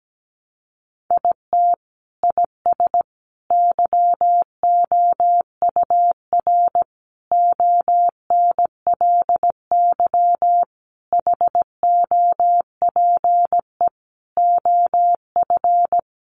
Morse code, heard as a keyed tone, about 17 wpm, 700 Hz. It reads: IT IS YOUR ONLY HOPE OF
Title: Sentences from Top 1000 Words 17wpm